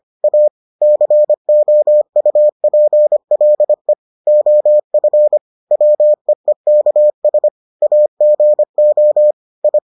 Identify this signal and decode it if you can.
Morse code, heard as a keyed tone, about 25 wpm, 600 Hz. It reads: A COUPLE OF WEEKS AGO I